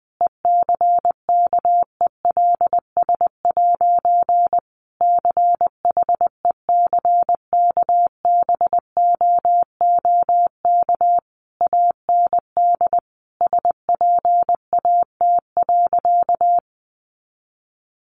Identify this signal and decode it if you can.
Morse code, heard as a keyed tone, about 20 wpm, 700 Hz. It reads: ECKELS' CHECKBOOK AND SPAT.
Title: A Sound Of Thunder 20wpm